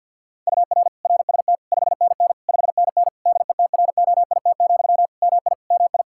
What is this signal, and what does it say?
Morse code, heard as a keyed tone, about 50 wpm, 700 Hz. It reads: UR RST 5NN 5NN DETROIT, MI MI
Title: Practice QSO - Exchange 1 - Beginning - 50wpm